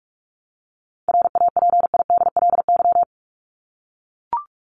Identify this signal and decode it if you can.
Morse code, heard as a keyed tone, about 44 wpm, 700 Hz. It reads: RAPIDLY
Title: LY Suffix - Top 100 Words with Base Words 45wpm